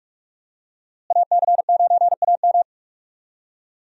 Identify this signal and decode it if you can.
Morse code, heard as a keyed tone, about 45 wpm, 700 Hz. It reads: AC9AM
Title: USA Call Signs - 5 Characters - Repeated 2x - 45wpm